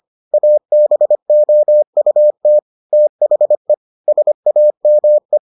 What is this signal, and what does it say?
Morse code, heard as a keyed tone, about 25 wpm, 600 Hz. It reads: ABOUT THE SAME